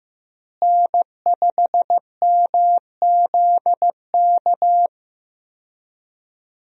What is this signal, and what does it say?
Morse code, heard as a keyed tone, about 15 wpm, 700 Hz. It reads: N5MZK
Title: USA Call Signs - 5 Characters - Repeated 2x - 15wpm